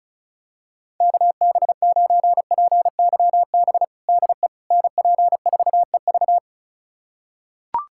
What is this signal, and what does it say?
Morse code, heard as a keyed tone, about 36 wpm, 700 Hz. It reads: KB9PYB DE NP4EV
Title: USA Callsign DE Callsign 35wpm